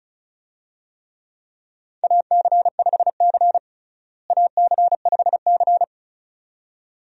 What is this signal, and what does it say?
Morse code, heard as a keyed tone, about 36 wpm, 700 Hz. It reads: AC5C AC5C
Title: USA Call Signs - 4 Characters - Repeated 2x - 35wpm